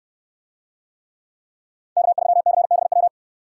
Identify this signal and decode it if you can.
Morse code, heard as a keyed tone, about 68 wpm, 700 Hz. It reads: K3CBW